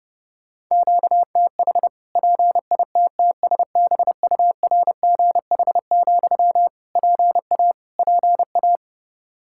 Morse code, heard as a keyed tone, about 31 wpm, 700 Hz. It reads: QTH PITTSBURGH, PA PA
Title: Practice QSO - Exchange 1 - Beginning - 30wpm